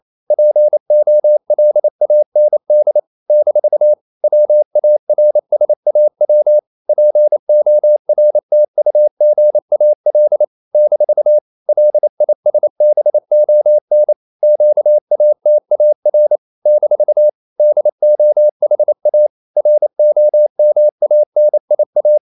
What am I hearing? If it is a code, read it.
Morse code, heard as a keyed tone, about 28 wpm, 600 Hz. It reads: POLAND - WARSAW PORTUGAL - LISBON QATAR - DOHA ROMANIA
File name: CW / Practice / NationalCapitals28